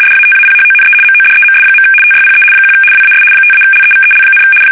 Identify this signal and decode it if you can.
RTTY, about 75 baud. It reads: DATAPHONE MODEMS. TYPE 1 WAS 5-LEVEL,
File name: fsk850-75B / teletype-corp